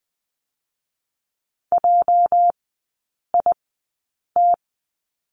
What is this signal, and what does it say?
Morse code, heard as a keyed tone, about 20 wpm, spaced as for 8 wpm, 700 Hz. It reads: JIT
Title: 205 Lesson 33e - Letter j - Three Characters 20wpm